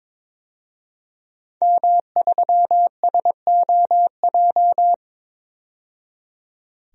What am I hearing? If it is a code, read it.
Morse code, heard as a keyed tone, about 22 wpm, 700 Hz. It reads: M3SOJ
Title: DXCC-100 Call Signs - 5 Characters - 22wpm